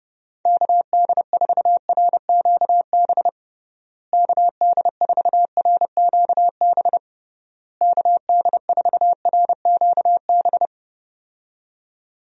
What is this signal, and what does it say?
Morse code, heard as a keyed tone, about 30 wpm, 700 Hz. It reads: KD4RQB KD4RQB KD4RQB
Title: USA Call Signs - 6 Characters - Repeated 3x - 30wpm